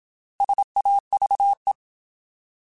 Morse code, heard as a keyed tone, about 27 wpm, 800 Hz. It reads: SAVE
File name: Alphabetical / Alphabetical mp3s / Ss @25